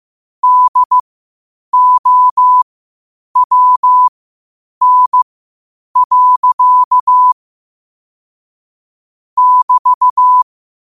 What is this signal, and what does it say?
Morse code, heard as a keyed tone, about 15 wpm, spaced as for 8 wpm, 1000 Hz. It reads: DOWN. <BT>
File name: qotdfiles / 8wpm